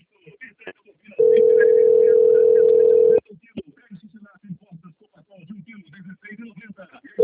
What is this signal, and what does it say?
Ringback tone (North America), 440 + 480 Hz, with speech.
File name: out- - - - - .9404